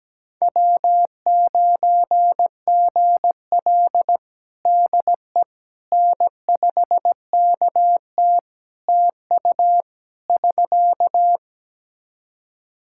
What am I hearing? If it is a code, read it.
Morse code, heard as a keyed tone, about 17 wpm, 700 Hz. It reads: W9GL DE N5KT TU <SK>
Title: Practice QSO - Exchange 4 - Ending - 17wpm